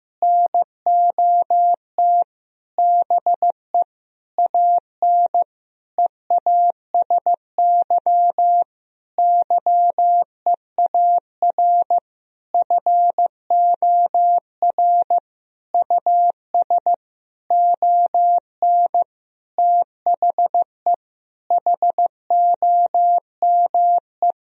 Morse code, heard as a keyed tone, about 15 wpm, 700 Hz. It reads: NOT BE AN EASY YEAR FOR US ON THE HOME
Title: Sentences from Top 1000 Words - Review Part 2 - 15wpm